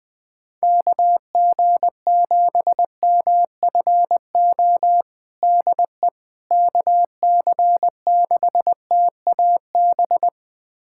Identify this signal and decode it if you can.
Morse code, heard as a keyed tone, about 21 wpm, 700 Hz. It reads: KG7MFO DE KC6TAB
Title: USA Callsign DE Callsign 20wpm